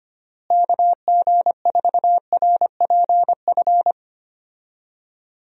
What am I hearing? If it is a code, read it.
Morse code, heard as a keyed tone, about 25 wpm, 700 Hz. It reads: KG4RPF
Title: USA Call Signs - 6 Characters - Repeated 3x - 25wpm